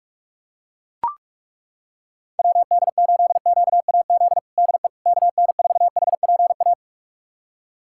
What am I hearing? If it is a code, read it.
Morse code, heard as a keyed tone, about 41 wpm, 700 Hz. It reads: WD8QAZ DE KN4HPA
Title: USA Callsign DE Callsign - 6 Characters - 45wpm